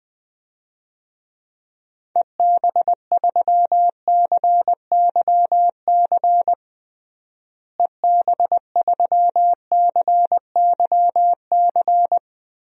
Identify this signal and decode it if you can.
Morse code, heard as a keyed tone, about 20 wpm, 700 Hz. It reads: EB3CYC EB3CYC
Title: DXCC-100 Call Signs - 6 Characters - Repeated 3x - 20wpm